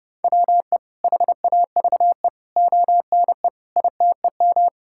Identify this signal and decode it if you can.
Morse code, heard as a keyed tone, about 30 wpm, 700 Hz. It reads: WE HAVE ONE ITEM
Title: Sentences from Top 1000 Words - Review Part 6 - 30wpm